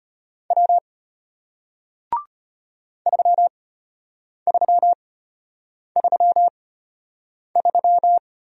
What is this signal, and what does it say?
Morse code, heard as a keyed tone, about 34 wpm, spaced as for 8 wpm, 700 Hz. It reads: W 3333